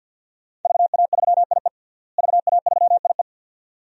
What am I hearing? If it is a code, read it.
Morse code, heard as a keyed tone, about 49 wpm, 700 Hz. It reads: VA3IE VA3IE
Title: DXCC-100 Call Signs - 5 Characters - Repeated 2x - 50wpm